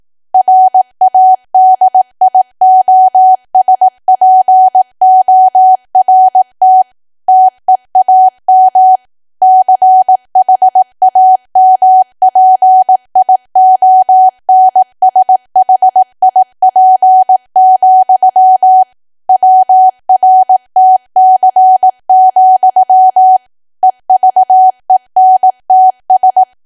Morse code, heard as a keyed tone, about 18 wpm, 750 Hz. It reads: RADIOSPORT TEAM CHAMPIONSHIP, WRTC, EVENTS
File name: Archive / 18 WPM / 18WPM